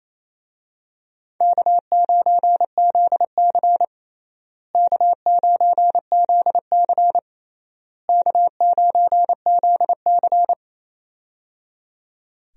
Morse code, heard as a keyed tone, about 28 wpm, 700 Hz. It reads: K9ZC K9ZC K9ZC
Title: USA Call Signs - 4 Characters - Repeated 3x - 28wpm